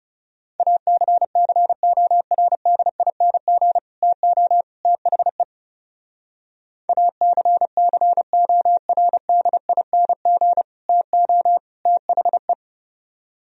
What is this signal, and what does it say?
Morse code, heard as a keyed tone, about 36 wpm, 700 Hz. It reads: ACCORDING TO THE ACCORDING TO THE
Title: Speed-Racing - Sets of 3 Words 30wpm